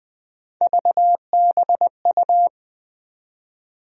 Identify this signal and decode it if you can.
Morse code, heard as a keyed tone, about 20 wpm, 700 Hz. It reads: VBU
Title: USA Call Sign Postfixes - 3 Letters 20wpm